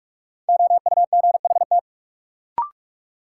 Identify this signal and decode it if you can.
Morse code, heard as a keyed tone, about 46 wpm, 700 Hz. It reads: OUGHT
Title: Sets of 5 Words Encore - New Word Drill - 45wpm